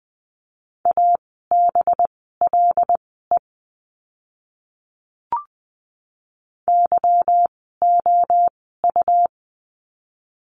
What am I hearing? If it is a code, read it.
Morse code, heard as a keyed tone, about 20 wpm, spaced as for 15 wpm, 700 Hz. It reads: ABLE YOU
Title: 185 Lesson 30f - Letter b - Words 20wpm